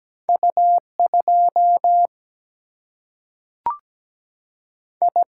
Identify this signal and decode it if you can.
Morse code, heard as a keyed tone, about 17 wpm, 700 Hz. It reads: U2 I
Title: Single-Letter Single-Number 17wpm